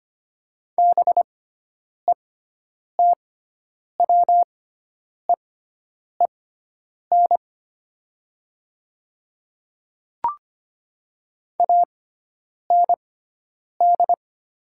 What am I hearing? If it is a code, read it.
Morse code, heard as a keyed tone, about 25 wpm, spaced as for 9 wpm, 700 Hz. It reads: BETWEEN AND
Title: 6x Character Spacing - Top 500 Words - 25wpm